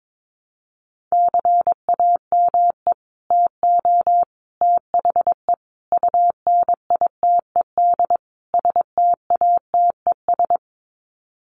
Morse code, heard as a keyed tone, about 22 wpm, 700 Hz. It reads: CAME TO THE UNITED STATES
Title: Sets of 5 Words Encore2 22wpm